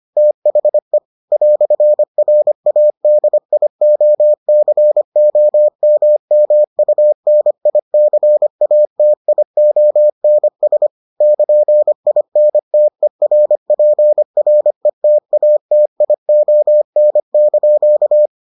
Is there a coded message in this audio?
Yes: Morse code from 0.2 to 18.3 s, heard as a keyed tone, about 25 wpm, 600 Hz. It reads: THE "RADIOCOMMUNICATIONS <KN>INTERPRETATION)